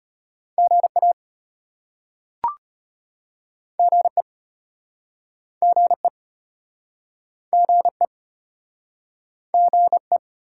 Morse code, heard as a keyed tone, about 32 wpm, 690 Hz. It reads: GA GE GE GE GE